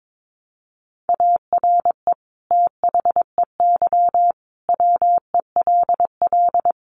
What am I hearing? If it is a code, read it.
Morse code, heard as a keyed tone, about 22 wpm, 700 Hz. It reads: ARE THEY WELL
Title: Sentences from Top 1300 Words 22wpm